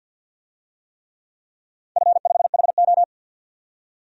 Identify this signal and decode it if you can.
Morse code, heard as a keyed tone, about 53 wpm, 700 Hz. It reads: F5HO
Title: DXCC-100 Call Signs - 4 Characters - 50wpm